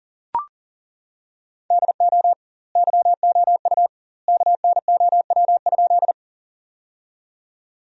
Morse code, heard as a keyed tone, about 37 wpm, 700 Hz. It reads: DO YOU KNOW?
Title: 2x Word-Spacing - Sentences from Top 100 Words - 40wpm